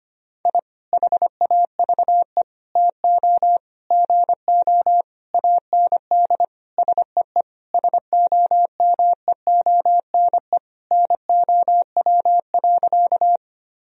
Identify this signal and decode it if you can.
Morse code, heard as a keyed tone, about 25 wpm, 700 Hz. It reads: I HAVE TO GO AND SEE SOMEONE NOW.
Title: Sentences from Top 1000 Words - Review Part 4 - 25wpm